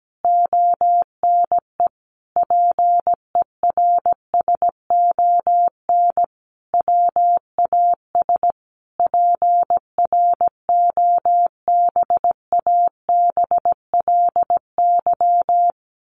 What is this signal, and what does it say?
Morse code, heard as a keyed tone, about 17 wpm, 700 Hz. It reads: ONE PERSON WAS PROBABLY
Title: Sentences from Top 1400 Words 17wpm